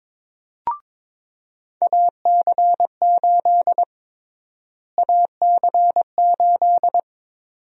Morse code, heard as a keyed tone, about 22 wpm, 700 Hz. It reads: AC8 AC8
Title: USA Call Sign Prefixes - Repeated 2x - 22wpm